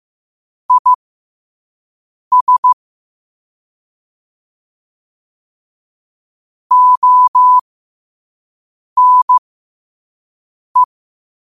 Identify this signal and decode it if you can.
Morse code, heard as a keyed tone, about 15 wpm, spaced as for 5 wpm, 1000 Hz. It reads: IS ONE
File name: qotdfiles / 5wpm